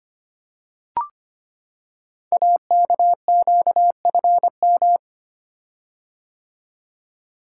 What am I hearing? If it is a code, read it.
Morse code, heard as a keyed tone, about 25 wpm, 700 Hz. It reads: AKQFM
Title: Code Groups - 5 Letter:Numbers - 25wpm